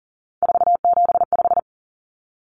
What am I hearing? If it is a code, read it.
Morse code, heard as a keyed tone, about 39 wpm, 700 Hz. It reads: 475
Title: Common Cut Numbers - 3-Digit 40wpm